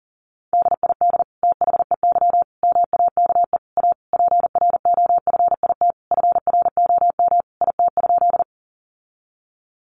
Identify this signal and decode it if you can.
Morse code, heard as a keyed tone, about 40 wpm, 700 Hz. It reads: DID THEY MAKE A PROFIT FROM IT?
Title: Sentences from Top 800 Words 40wpm